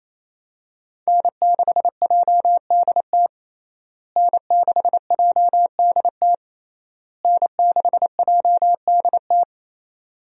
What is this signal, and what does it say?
Morse code, heard as a keyed tone, about 28 wpm, 700 Hz. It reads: N6JDT N6JDT N6JDT
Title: USA Call Signs - 5 Characters - Repeated 3x - 28wpm